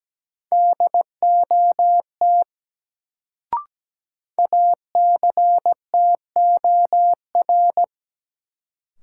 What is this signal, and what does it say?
Morse code, heard as a keyed tone, about 17 wpm, 700 Hz. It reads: DOT ACTOR
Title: Sentences from Top 1500 Words - New Word Drill - 17wpm